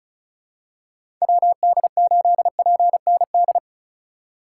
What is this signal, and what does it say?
Morse code, heard as a keyed tone, about 34 wpm, 700 Hz. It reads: WD8PND
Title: USA Call Signs - 4x Character Spacing - 35wpm